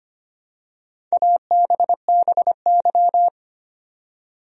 Morse code, heard as a keyed tone, about 25 wpm, 700 Hz. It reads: ABBY
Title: USA Common Names - Max 4 Letters - 25wpm